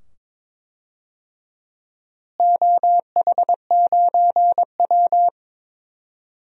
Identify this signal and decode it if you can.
Morse code, heard as a keyed tone, about 22 wpm, 700 Hz. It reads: OH9W
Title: DXCC-100 Call Signs - 4 Characters - Repeated 2x - 22wpm